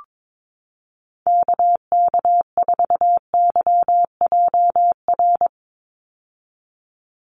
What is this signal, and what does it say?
Morse code, heard as a keyed tone, about 22 wpm, 700 Hz. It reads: KK4YJR
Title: USA Call Signs 22wpm